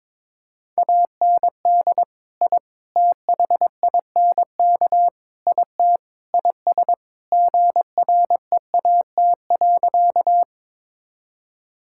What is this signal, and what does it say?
Morse code, heard as a keyed tone, about 22 wpm, 700 Hz. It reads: AND I THINK IT IS GREAT.
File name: practice / 22wpm / Sentences with 4-Word Phrases - 22wpm